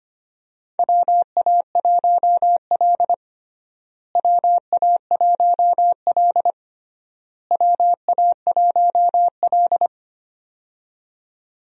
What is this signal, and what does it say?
Morse code, heard as a keyed tone, about 25 wpm, 700 Hz. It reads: WA1L WA1L WA1L
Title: USA Call Signs - 4 Characters - Repeated 3x - 25wpm